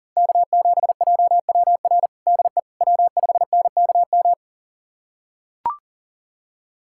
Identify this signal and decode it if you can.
Morse code, heard as a keyed tone, about 37 wpm, 700 Hz. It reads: K7JWR DE W5NKM
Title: USA Callsign DE Callsign 40wpm